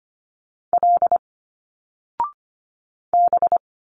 Morse code, heard as a keyed tone, about 25 wpm, 700 Hz. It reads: L B
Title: Single Letters 25wpm